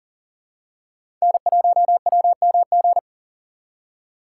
Morse code, heard as a keyed tone, about 43 wpm, 700 Hz. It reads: N1WMG
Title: USA Call Signs - 5 Characters - 6x Character Spacing - 40wpm